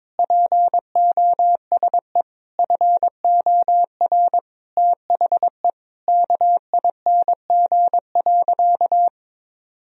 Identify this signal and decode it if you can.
Morse code, heard as a keyed tone, about 22 wpm, 700 Hz. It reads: POSE FOR THE KING.
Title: Sentences from Top 1000 Words - Review Part 2 - 22wpm